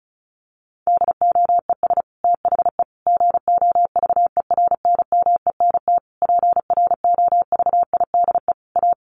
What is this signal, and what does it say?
Morse code, heard as a keyed tone, about 34 wpm, 700 Hz. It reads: DOES THE GOVERNMENT PROVIDE A
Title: Sentences from Top 1500 Words 35wpm